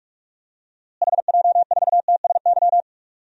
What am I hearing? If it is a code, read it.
Morse code, heard as a keyed tone, about 46 wpm, 700 Hz. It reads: HJ4TSY
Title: DXCC-100 Call Signs - 6 Characters - Repeated 3x - 45wpm